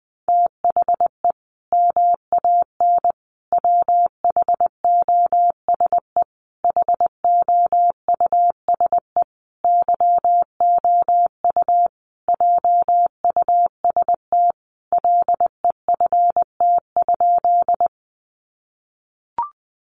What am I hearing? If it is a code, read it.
Morse code, heard as a keyed tone, about 20 wpm, 700 Hz. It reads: THE MAN WHOSE HOUSE YOU JUST LEFT?
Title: Sentences from Top 1200 Words 20wpm